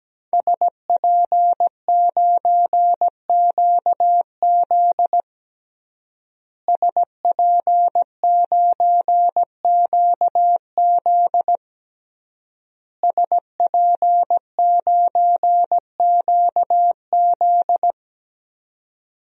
Morse code, heard as a keyed tone, about 17 wpm, 700 Hz. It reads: SP9QZ SP9QZ SP9QZ
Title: DXCC-100 Call Signs - 5 Characters - Repeated 3x - 17wpm